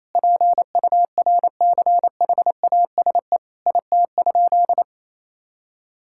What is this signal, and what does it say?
Morse code, heard as a keyed tone, about 28 wpm, 700 Hz. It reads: PURCHASE IT?
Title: Sentences from Top 1000 Words - Review Part 5 - 28wpm